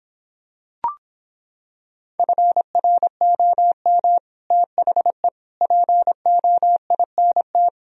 Morse code, heard as a keyed tone, about 26 wpm, 700 Hz. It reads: FROM THE POINT